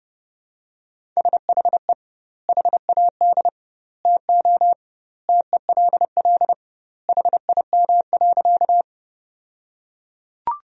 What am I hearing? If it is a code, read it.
Morse code, heard as a keyed tone, about 30 wpm, 700 Hz. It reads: SHE HAD TO TELL HIM.
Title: 2x Word-Spacing - Sentences from Top 200 Words - 30wpm